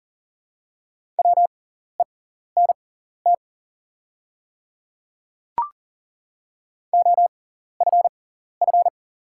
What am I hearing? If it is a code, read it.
Morse code, heard as a keyed tone, about 39 wpm, spaced as for 14 wpm, 700 Hz. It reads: WENT OFF
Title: 6x Character Spacing - Top 200 Words - 40wpm